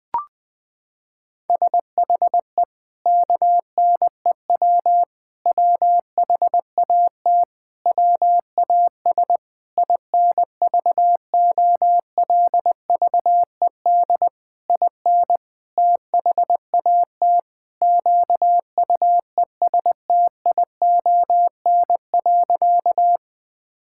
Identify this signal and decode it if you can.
Morse code, heard as a keyed tone, about 20 wpm, 700 Hz. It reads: SHE KNEW WHAT WAS INVOLVED IN THAT QUESTION.
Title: Sentences from Top 1000 Words - Review Part 5 - 20wpm